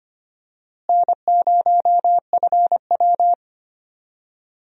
Morse code, heard as a keyed tone, about 25 wpm, 700 Hz. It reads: N0FW
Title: USA Call Signs - 4 Characters - Repeated 2x - 25wpm